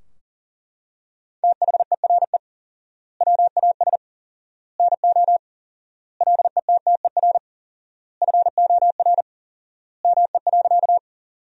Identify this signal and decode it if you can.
Morse code, heard as a keyed tone, about 40 wpm, 700 Hz. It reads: THERE WAS NO LETTER FOR ME.
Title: 4x Word-Spacing - Sentences from Top 300 Words - 40wpm